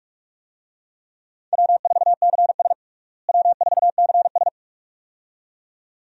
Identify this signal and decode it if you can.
Morse code, heard as a keyed tone, about 46 wpm, 700 Hz. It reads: W4CS W4CS
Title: USA Call Signs - 4 Characters - Repeated 2x - 45wpm